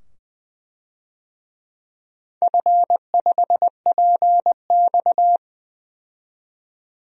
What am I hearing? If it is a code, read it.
Morse code, heard as a keyed tone, about 20 wpm, 700 Hz. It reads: F5PX
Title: DXCC-100 Call Signs - 4 Characters - 20wpm